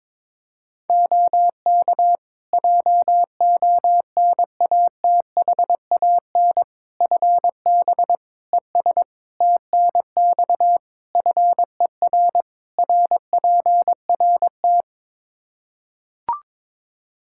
Morse code, heard as a keyed tone, about 22 wpm, 700 Hz. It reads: OK JONATHAN FB ES TNX FER RPRT
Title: Practice QSO - Exchange 2 - Info - 22wpm